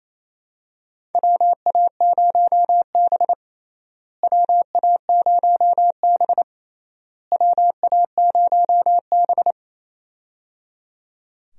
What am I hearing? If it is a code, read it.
Morse code, heard as a keyed tone, about 27 wpm, 700 Hz. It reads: WA0B WA0B WA0B
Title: USA Call Signs - 4 Characters - Repeated 3x - 28wpm